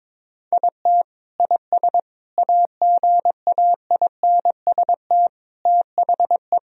Morse code, heard as a keyed tone, about 22 wpm, 700 Hz. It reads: IT IS AGAINST THE
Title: Sentences from Top 1000 Words - Review Part 3 - 22wpm